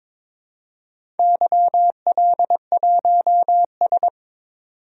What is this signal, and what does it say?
Morse code, heard as a keyed tone, about 22 wpm, 700 Hz. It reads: YL1S
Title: DXCC-100 Call Signs - 4 Characters - 22wpm